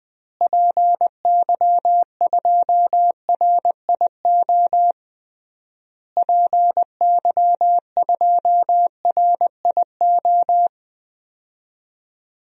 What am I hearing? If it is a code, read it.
Morse code, heard as a keyed tone, about 20 wpm, 700 Hz. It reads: PY2RIO PY2RIO
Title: DXCC-100 Call Signs - 6 Characters - Repeated 3x - 20wpm